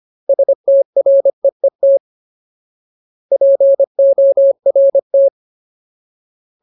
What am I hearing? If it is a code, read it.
Morse code, heard as a keyed tone, about 25 wpm, 550 Hz. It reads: STREET PORT
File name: Ultimate Word List / 25WPM Ultimate